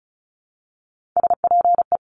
Morse code, heard as a keyed tone, about 36 wpm, 690 Hz. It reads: SPE
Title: Top Three Letter Combinations in Words 35wpm